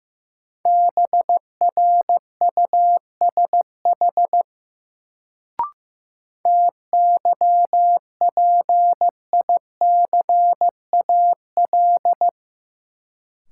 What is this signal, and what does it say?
Morse code, heard as a keyed tone, about 15 wpm, 700 Hz. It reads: BRUSH TYPICAL
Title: Sentences from Top 1100 Words - New Word Drill - 15wpm